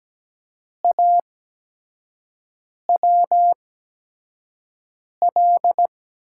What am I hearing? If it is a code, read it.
Morse code, heard as a keyed tone, about 17 wpm, spaced as for 5 wpm, 700 Hz. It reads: AWL
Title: USA Call Signs - 6 Characters - 8x Character Spacing - 17wpm